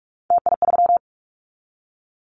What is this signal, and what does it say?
Morse code, heard as a keyed tone, about 46 wpm, 700 Hz. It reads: TI3
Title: Callsign Prefixes for Top 100 DXCC - 45wpm